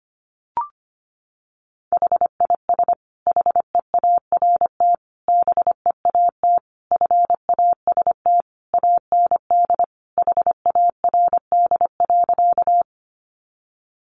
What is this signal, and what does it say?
Morse code, heard as a keyed tone, about 25 wpm, 700 Hz. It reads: HIS HEART BEAT FAST AND HARD.
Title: Sentences from Top 1300 Words 25wpm